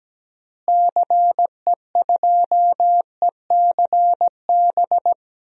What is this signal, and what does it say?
Morse code, heard as a keyed tone, about 17 wpm, 700 Hz. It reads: CE2ECB
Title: DXCC-100 Call Signs - 6 Characters - Repeated 3x - 17wpm